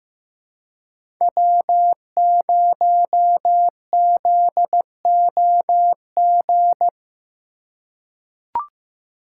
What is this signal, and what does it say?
Morse code, heard as a keyed tone, about 15 wpm, 700 Hz. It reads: W0ZOG
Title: USA Call Signs - 5 Characters - 4x Character Spacing - 15wpm